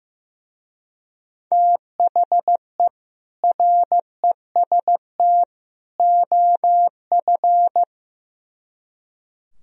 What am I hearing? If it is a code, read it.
Morse code, heard as a keyed tone, about 17 wpm, 700 Hz. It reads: THE REST OF
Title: Speed-Racing - Sets of 3 Words 15wpm